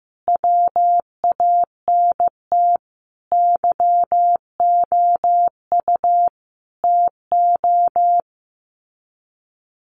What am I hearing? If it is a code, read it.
Morse code, heard as a keyed tone, about 15 wpm, 700 Hz. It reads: WANT YOU TO
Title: Sets of 5 Words Encore2 15wpm